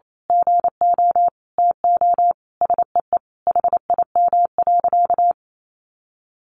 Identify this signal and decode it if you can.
Morse code, heard as a keyed tone, about 28 wpm, 700 Hz. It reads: GO TO SEE HIM.
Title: Sentences from Top 1500 Words 28wpm